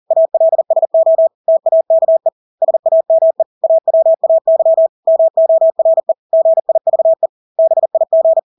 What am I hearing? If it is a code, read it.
Morse code, heard as a keyed tone, about 40 wpm, 650 Hz. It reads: ALSO TAKE SAME AWAY MORE GIVE BIG
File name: Engelske ord 40-40 wpm